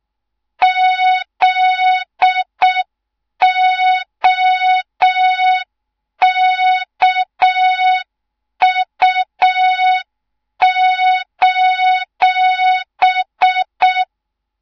Morse code, heard as a keyed tone, about 6 wpm, 3700 Hz. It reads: ZOKU:
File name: Morze-HA2MN / hang / 24 GYAKORLAT